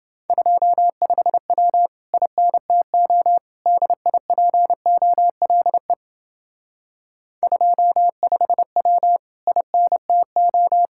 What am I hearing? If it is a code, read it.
Morse code, heard as a keyed tone, about 28 wpm, 700 Hz. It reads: 25W INTO DIPOLE 25W INTO